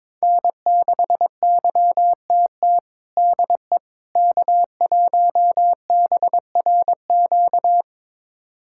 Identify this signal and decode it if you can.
Morse code, heard as a keyed tone, about 21 wpm, 700 Hz. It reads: N6YTT DE K1BRQ
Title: USA Callsign DE Callsign 22wpm